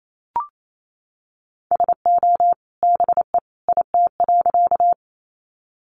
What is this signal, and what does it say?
Morse code, heard as a keyed tone, about 28 wpm, 700 Hz. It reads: SO BE IT.
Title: Sentences from Top 100 Words 28wpm